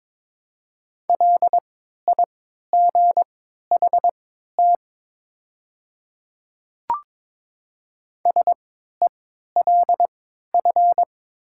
Morse code, heard as a keyed tone, about 22 wpm, spaced as for 13 wpm, 700 Hz. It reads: LIGHT SELF
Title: 3x Character Spacing - Top 200 Words - 22wpm